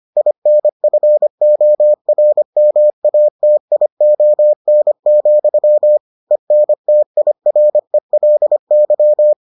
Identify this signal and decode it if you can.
Morse code, heard as a keyed tone, about 25 wpm, 600 Hz. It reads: INFORMATION, ENTIRELY